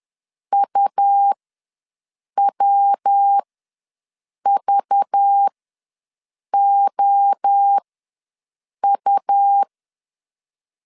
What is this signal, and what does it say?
Morse code, heard as a keyed tone, about 10 wpm, spaced as for 6 wpm, 800 Hz. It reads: UWVOU
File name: LEZIONI / 6 PKX0 / 6 PKX ws 10wpm 3706